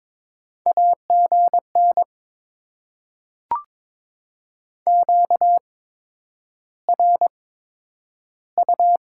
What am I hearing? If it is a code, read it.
Morse code, heard as a keyed tone, about 22 wpm, spaced as for 9 wpm, 700 Hz. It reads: AGN QRU
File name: Top 90 QSO Elements - 8x Character Spacing - 22wpm